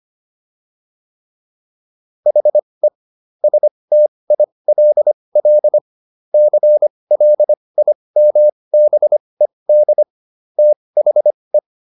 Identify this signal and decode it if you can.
Morse code, heard as a keyed tone, about 25 wpm, spaced as for 20 wpm, 600 Hz. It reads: HE STILL CLIMBED THE